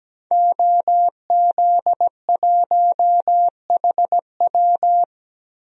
Morse code, heard as a keyed tone, about 17 wpm, 700 Hz. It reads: OZ1HW
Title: DXCC-100 Call Signs - 5 Characters - Repeated 3x - 17wpm